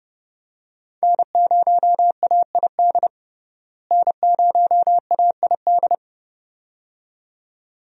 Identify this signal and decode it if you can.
Morse code, heard as a keyed tone, about 30 wpm, 700 Hz. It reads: N0AID N0AID
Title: USA Call Signs - 5 Characters - Repeated 2x - 30wpm